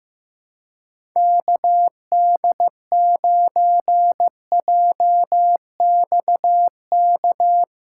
Morse code, heard as a keyed tone, about 15 wpm, 700 Hz. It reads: KD9JXK
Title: USA Callsign DE Callsign - 6 Characters - 15wpm